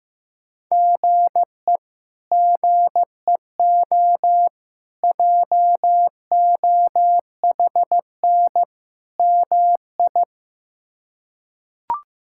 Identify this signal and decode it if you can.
Morse code, heard as a keyed tone, about 15 wpm, 700 Hz. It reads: GE GEO JOHN MI
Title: K1USN SST QSO Practice - 15wpm